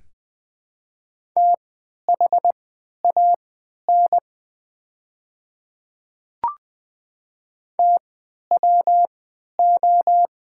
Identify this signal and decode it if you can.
Morse code, heard as a keyed tone, about 20 wpm, spaced as for 11 wpm, 700 Hz. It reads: THAN TWO
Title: 3x Character Spacing - Top 500 Words - 20wpm